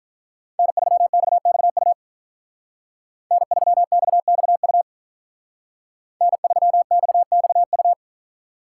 Morse code, heard as a keyed tone, about 47 wpm, 700 Hz. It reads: N3XXU N3XXU N3XXU